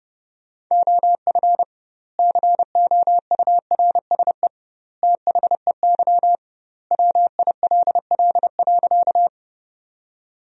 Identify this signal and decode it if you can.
Morse code, heard as a keyed tone, about 32 wpm, 700 Hz. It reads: OF COURSE THEY WILL.
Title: 2x Word-Spacing - Sentences from Top 500 Words - 30wpm